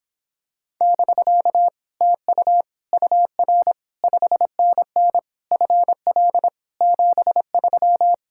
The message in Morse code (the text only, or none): <BK> TU UR 5NN FL 73